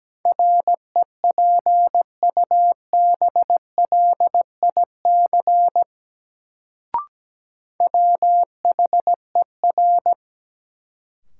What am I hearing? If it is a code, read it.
Morse code, heard as a keyed tone, about 17 wpm, 700 Hz. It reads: REPUBLIC WHER
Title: Sentences from Top 1500 Words - New Word Drill - 17wpm